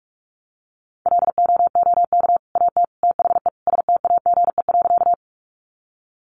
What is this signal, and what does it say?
Morse code, heard as a keyed tone, about 44 wpm, 700 Hz. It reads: LOOK AT THE STAGE.
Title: Sentences from Top 1200 Words 45wpm